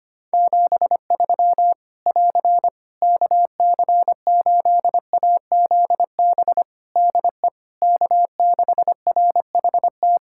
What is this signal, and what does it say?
Morse code, heard as a keyed tone, about 25 wpm, 700 Hz. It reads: 73 <AR> KC8AZB DE K6RHT
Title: Practice QSO - Exchange 4 - Ending - 25wpm